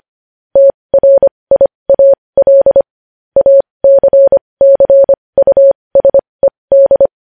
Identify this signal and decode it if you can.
Morse code, heard as a keyed tone, about 25 wpm, spaced as for 20 wpm, 550 Hz. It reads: TRIAL ACCUSED
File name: podcast / News headlines-1x-25x20